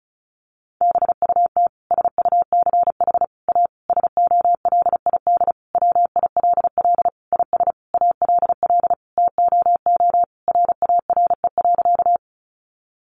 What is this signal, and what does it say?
Morse code, heard as a keyed tone, about 34 wpm, 700 Hz. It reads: BUT SUCH A SOLID WILL IS ALL TOO RARE.
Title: Sentences from Top 1500 Words 35wpm